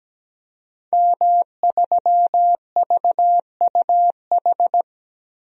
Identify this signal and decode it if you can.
Morse code, heard as a keyed tone, about 17 wpm, 700 Hz. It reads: M3VUH
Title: DXCC-100 Call Signs - 5 Characters - 17wpm